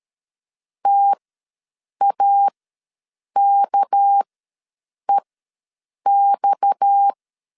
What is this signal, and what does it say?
Morse code, heard as a keyed tone, about 13 wpm, spaced as for 7 wpm, 800 Hz. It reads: TAKEX
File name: LEZIONI / 6 PKX0 / 6 PKX ws 12wpm 3106